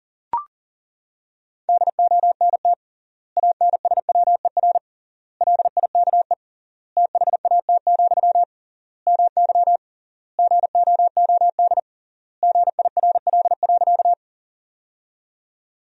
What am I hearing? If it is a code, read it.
Morse code, heard as a keyed tone, about 40 wpm, 700 Hz. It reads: DONT ANSWER LIKE THAT, MY GOOD GIRL.
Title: 3x Word-Spacing - Sentences from Top 400 Words - 40wpm